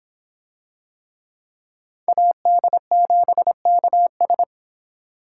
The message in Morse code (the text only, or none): AD7KS